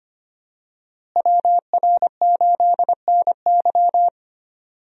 Morse code, heard as a keyed tone, about 25 wpm, 700 Hz. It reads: WR8NY
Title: USA Call Signs - 5 Characters - Repeated 2x - 25wpm